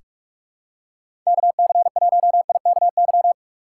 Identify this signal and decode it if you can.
Morse code, heard as a keyed tone, about 41 wpm, 700 Hz. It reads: KC1IKY
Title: USA Call Signs - 6x Character Spacing - 45wpm